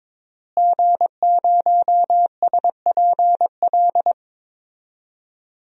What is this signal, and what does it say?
Morse code, heard as a keyed tone, about 22 wpm, 700 Hz. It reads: G0SPL
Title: DXCC-100 Call Signs - 5 Characters - Repeated 3x - 22wpm